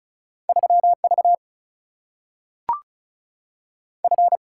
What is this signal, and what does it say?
Morse code, heard as a keyed tone, about 36 wpm, 700 Hz. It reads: 3 V F
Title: Single-Letter Single-Number 35wpm